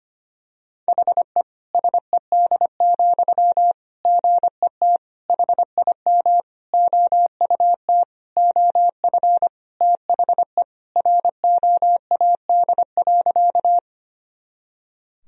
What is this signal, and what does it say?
Morse code, heard as a keyed tone, about 25 wpm, 700 Hz. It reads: HE SED, GET HIM OUT OF THE ROAD.
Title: Sentences from Top 400 Words and Abbreviations 25wpm